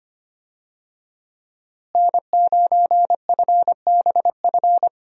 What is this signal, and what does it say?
Morse code, heard as a keyed tone, about 25 wpm, 700 Hz. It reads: N9FBF
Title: USA Call Signs - 5 Characters - Repeated 2x - 25wpm